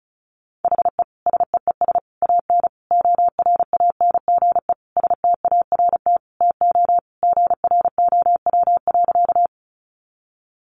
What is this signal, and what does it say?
Morse code, heard as a keyed tone, about 34 wpm, 700 Hz. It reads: HE SEES AN ORANGE START TO GROW.
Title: Easy Phrases 35wpm